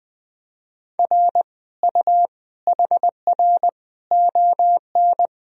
Morse code, heard as a keyed tone, about 21 wpm, 700 Hz. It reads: R U HR ON
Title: Sentences from Top 400 Words and Abbreviations 20wpm